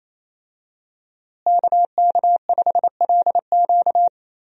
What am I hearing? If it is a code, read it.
Morse code, heard as a keyed tone, about 27 wpm, 700 Hz. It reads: KK5LQ
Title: USA Call Signs - 5 Characters - Repeated 2x - 28wpm